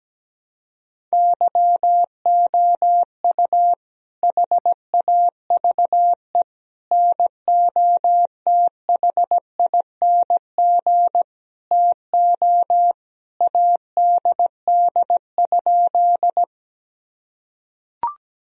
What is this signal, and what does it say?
Morse code, heard as a keyed tone, about 17 wpm, 700 Hz. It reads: YOU HAVE NOTHING TO ADD?
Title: Sentences from Top 1000 Words - Review Part 1 - 17wpm